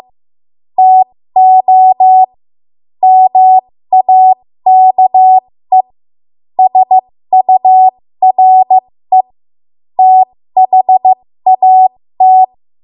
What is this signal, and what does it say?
Morse code, heard as a keyed tone, about 15 wpm, spaced as for 13 wpm, 750 Hz. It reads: TO MAKE SURE THAT
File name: Morse / 13WPM